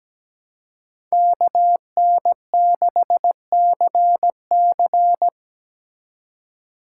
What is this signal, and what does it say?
Morse code, heard as a keyed tone, about 17 wpm, 700 Hz. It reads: KN6CC
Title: USA Call Signs - 5 Characters - 4x Character Spacing - 17wpm